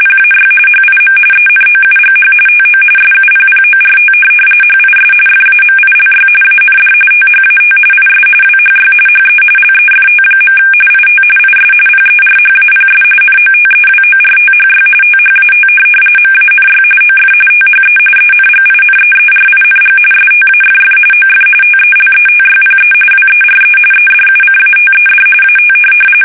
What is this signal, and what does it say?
RTTY, about 45.45 baud. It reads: PAY HIM ENOUGH TO SUPPORT HIS FAMILY AND THAT IS WHY HE LEFT. IT IS PERHAPS NOTEWORTHY THAT MANY OF THE ENGINEERING STAFF OF TELETYPE WERE EDUCATED AT
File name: fsk850 / teletype-corp